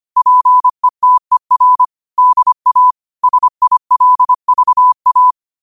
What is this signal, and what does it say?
Morse code, heard as a keyed tone, about 25 wpm, 1000 Hz. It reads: PETER DA SILVA